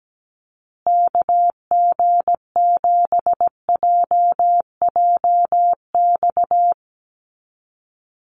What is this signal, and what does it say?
Morse code, heard as a keyed tone, about 17 wpm, 700 Hz. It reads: KG7JJX
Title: USA Call Signs Repeated 2x 17wpm